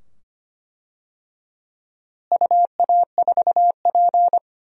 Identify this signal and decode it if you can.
Morse code, heard as a keyed tone, about 25 wpm, 700 Hz. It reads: UA4P
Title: DXCC-100 Call Signs - 4 Characters - Repeated 2x - 25wpm